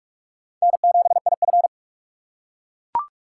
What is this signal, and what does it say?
Morse code, heard as a keyed tone, about 39 wpm, 700 Hz. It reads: N7IF